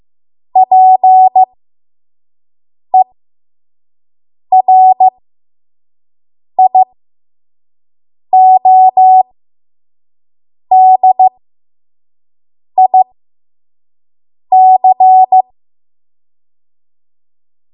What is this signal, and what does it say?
Morse code, heard as a keyed tone, about 15 wpm, spaced as for 5 wpm, 750 Hz. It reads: PERIODIC
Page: 5 WPM Code Archive